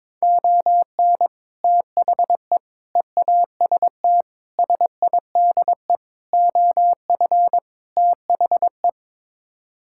Morse code, heard as a keyed tone, about 22 wpm, 700 Hz. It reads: ON THE EAST SIDE OF THE
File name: practice / 22wpm / Sets of 6 Words - 22wpm